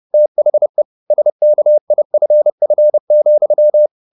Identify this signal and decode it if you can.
Morse code, heard as a keyed tone, about 30 wpm, 600 Hz. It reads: THE SKIFF,
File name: the old man and the sea / the old man and the sea 1 30wpm